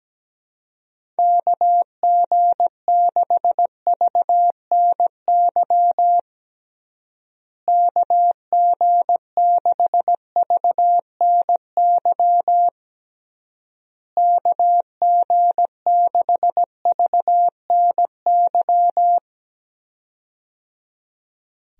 Morse code, heard as a keyed tone, about 17 wpm, 700 Hz. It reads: KG6VNY KG6VNY KG6VNY
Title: USA Call Signs - 6 Characters - Repeated 3x - 17wpm